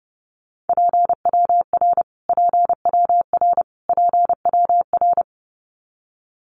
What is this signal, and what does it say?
Morse code, heard as a keyed tone, about 30 wpm, 700 Hz. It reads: PWR PWR PWR
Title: Abbreviations Repeated 3x - 30wpm